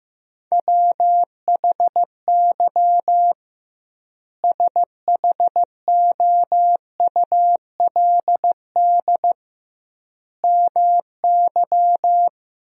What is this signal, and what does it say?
Morse code, heard as a keyed tone, about 15 wpm, 700 Hz. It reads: WHY SHOULD MY
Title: 2x Word-Spacing - Sentences from Top 500 Words - 15wpm